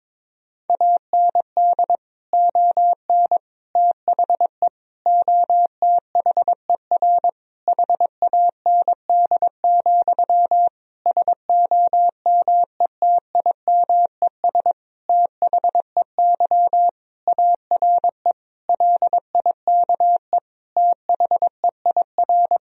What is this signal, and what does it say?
Morse code, heard as a keyed tone, about 21 wpm, 700 Hz. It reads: AND ON THE OTHER HAND, SOMETIMES THEY ARE LIKE THEIR
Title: Sentences from Top 1000 Words - Review Part 3 - 22wpm